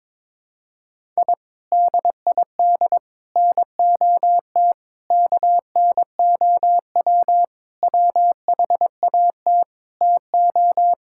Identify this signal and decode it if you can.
Morse code, heard as a keyed tone, about 22 wpm, 700 Hz. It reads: I DID NOT KNOW WHAT TO
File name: practice / 22wpm / Sentences with 4-Word Phrases - 22wpm